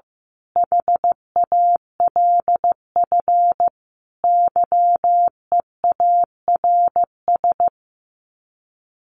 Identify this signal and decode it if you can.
Morse code, heard as a keyed tone, about 15 wpm, 700 Hz. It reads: HALF YEARS
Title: Sets of 5 Words 15wpm